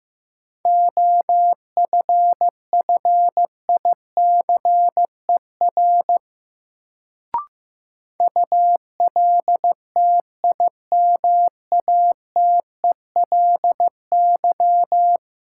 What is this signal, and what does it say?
Morse code, heard as a keyed tone, about 16 wpm, 700 Hz. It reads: OFFICER ULTIMATELY
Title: Sentences from Top 1200 Words - New Word Drill - 15wpm